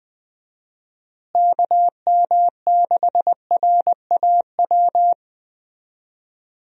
Morse code, heard as a keyed tone, about 20 wpm, 700 Hz. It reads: KM6RAW
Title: USA Call Signs - 6 Characters - Repeated 3x - 20wpm